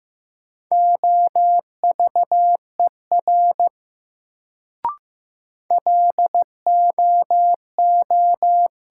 Morse code, heard as a keyed tone, about 16 wpm, 700 Hz. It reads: OVER LOO
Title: Sentences from Top 100 Words - New Word Drill - 15wpm